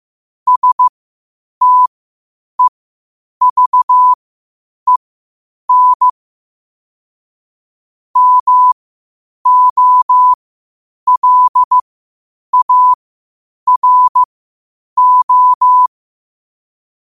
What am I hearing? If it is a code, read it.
Morse code, heard as a keyed tone, about 15 wpm, spaced as for 8 wpm, 1000 Hz. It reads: STEVEN MOLARO